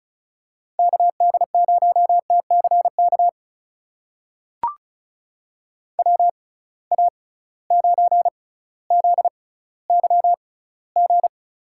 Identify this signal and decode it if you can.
Morse code, heard as a keyed tone, about 36 wpm, spaced as for 18 wpm, 700 Hz. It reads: KD0TCK WA9ZYG
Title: USA Call Signs - 6 Characters - 6x Character Spacing - 35wpm